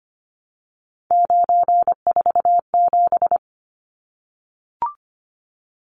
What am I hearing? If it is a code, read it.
Morse code, heard as a keyed tone, about 25 wpm, 700 Hz. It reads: 947
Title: Numbers - 3-Digit 25wpm